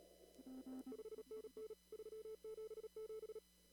Morse code, heard as a keyed tone, about 37 wpm, 440 Hz. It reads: 5NN 377
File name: CW / kt5j wpx cw-op-k5pi